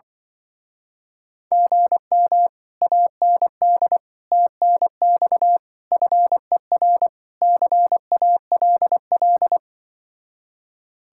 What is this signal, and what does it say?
Morse code, heard as a keyed tone, about 25 wpm, 700 Hz. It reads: GM AND TNX FER CALL
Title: Speed-Racing Common QSO Phrases - 20wpm